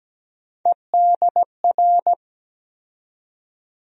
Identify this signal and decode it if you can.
Morse code, heard as a keyed tone, about 17 wpm, 700 Hz. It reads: EDR
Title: USA Call Sign Postfixes - 3 Letters 17wpm